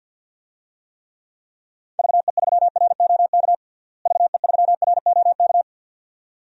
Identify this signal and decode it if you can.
Morse code, heard as a keyed tone, about 52 wpm, 700 Hz. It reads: VE3ROK VE3ROK
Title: DXCC-100 Call Signs - 6 Characters - Repeated 2x - 50wpm